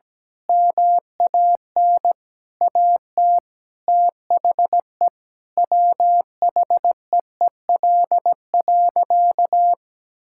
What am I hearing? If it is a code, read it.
Morse code, heard as a keyed tone, about 17 wpm, 700 Hz. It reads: MAN AT THE WHEEL.
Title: Sentences from Top 1000 Words - Review Part 5 - 17wpm